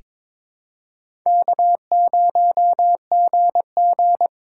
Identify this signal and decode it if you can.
Morse code, heard as a keyed tone, about 21 wpm, 690 Hz. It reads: K0GG
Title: USA Call Signs - 4 Characters - 3x Character Spacing - 22wpm